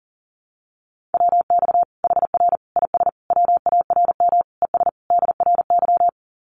Morse code, heard as a keyed tone, about 40 wpm, 700 Hz. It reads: WX HR IS WARM ES DRY
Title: Common QSO Phrases 40wpm